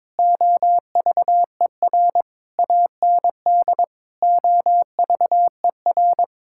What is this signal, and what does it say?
Morse code, heard as a keyed tone, about 22 wpm, 700 Hz. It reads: OVER AND OVER
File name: practice / 22wpm / Sentences with 4-Word Phrases - 22wpm